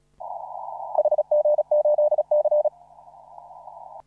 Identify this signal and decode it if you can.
Morse code, heard as a keyed tone, about 36 wpm, 600 Hz. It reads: HG8C
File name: CW / RHOB CW 2017 HA6P 0001